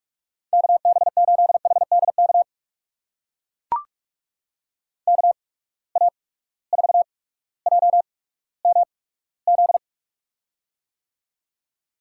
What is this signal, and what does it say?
Morse code, heard as a keyed tone, about 44 wpm, spaced as for 19 wpm, 700 Hz. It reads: KB8HDK KA4JMZ
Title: USA Call Signs - 6 Characters - 8x Character Spacing - 45wpm